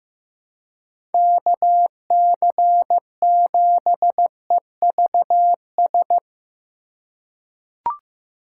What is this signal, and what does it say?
Morse code, heard as a keyed tone, about 15 wpm, 700 Hz. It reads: KC7EVS
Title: USA Call Signs - 6 Characters - 3x Character Spacing - 15wpm